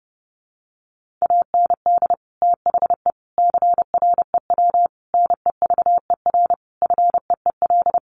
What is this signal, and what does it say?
Morse code, heard as a keyed tone, about 30 wpm, 700 Hz. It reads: AND THE CREW NEVER FEEL
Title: Sentences from Top 1500 Words 30wpm